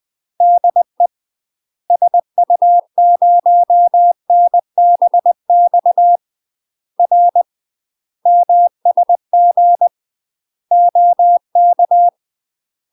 Morse code, heard as a keyed tone, about 20 wpm, 700 Hz. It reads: DE SU0NBX R MSG OK
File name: UK MASTER QSOs TRAINING/20WPM